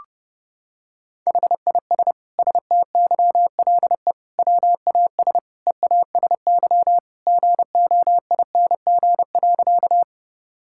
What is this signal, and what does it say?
Morse code, heard as a keyed tone, about 30 wpm, 700 Hz. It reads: HIS STYLE WAS EASY GOING.
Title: Sentences from Top 1000 Words - Review Part 5 - 30wpm